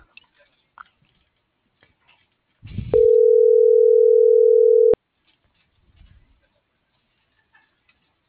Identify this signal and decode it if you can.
Ringback tone (North America), 440 + 480 Hz.